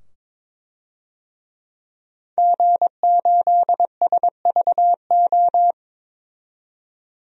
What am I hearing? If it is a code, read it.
Morse code, heard as a keyed tone, about 22 wpm, 700 Hz. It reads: G8SVO
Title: DXCC-100 Call Signs - 5 Characters - 22wpm